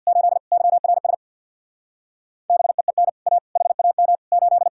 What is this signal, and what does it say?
Morse code, heard as a keyed tone, about 64 wpm, 700 Hz. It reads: 8 YRS BEEN A HAM 8